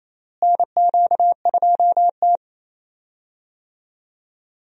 Morse code, heard as a keyed tone, about 28 wpm, 700 Hz. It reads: NQ2T
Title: USA Call Signs - 4 Characters - Repeated 3x - 28wpm